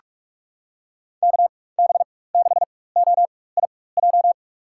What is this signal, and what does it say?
Morse code, heard as a keyed tone, about 43 wpm, spaced as for 21 wpm, 700 Hz. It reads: KB6OIJ
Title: USA Call Signs - 6 Characters - 4x Character Spacing - 45wpm